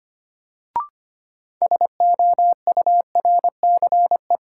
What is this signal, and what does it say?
Morse code, heard as a keyed tone, about 25 wpm, 700 Hz. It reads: SOURCE
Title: Sentences from Top 800 Words - New Word Drill - 25wpm